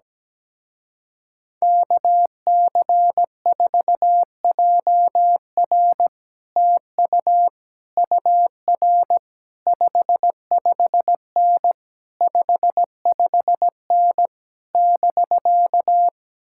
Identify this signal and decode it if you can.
Morse code, heard as a keyed tone, about 17 wpm, 700 Hz. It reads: KC4JR TU UR 55N 55N <BK>
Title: Speed-Racing Common QSO Phrases - 17wpm